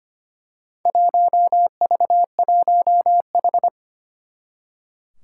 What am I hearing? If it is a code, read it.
Morse code, heard as a keyed tone, about 25 wpm, 700 Hz. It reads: 1V1H
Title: Four Challenging Letter-Number Combinations 25wpm